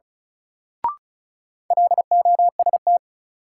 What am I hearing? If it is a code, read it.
Morse code, heard as a keyed tone, about 34 wpm, 700 Hz. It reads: LOST
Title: Sentences from Top 1400 Words - New Word Drill - 35wpm